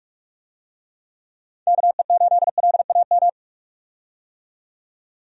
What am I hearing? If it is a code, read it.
Morse code, heard as a keyed tone, about 43 wpm, 700 Hz. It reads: KE8LAM
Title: USA Call Signs - 6 Characters - 45wpm